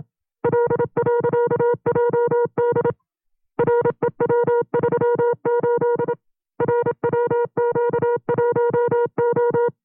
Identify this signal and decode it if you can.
Morse code, heard as a keyed tone, about 27 wpm, 480 Hz. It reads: L.JD REW38 RWQ1O
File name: CW-practice/25WPM